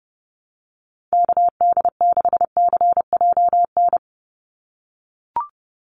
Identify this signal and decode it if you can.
Morse code, heard as a keyed tone, about 30 wpm, 700 Hz. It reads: KD6CJN
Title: USA Call Signs 30wpm